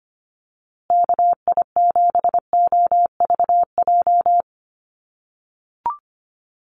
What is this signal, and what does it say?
Morse code, heard as a keyed tone, about 25 wpm, 700 Hz. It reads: KI7OVJ
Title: USA Call Signs Repeated 3x 25wpm